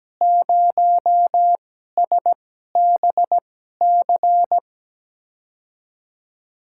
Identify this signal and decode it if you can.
Morse code, heard as a keyed tone, about 17 wpm, spaced as for 12 wpm, 700 Hz. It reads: 0SBC
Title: USA Call Signs - 5 Characters - 2x Character Spacing - 17wpm